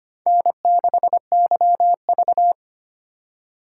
Morse code, heard as a keyed tone, about 25 wpm, 700 Hz. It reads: N6YV
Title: USA Call Signs - 4 Characters - 2x Character Spacing - 25wpm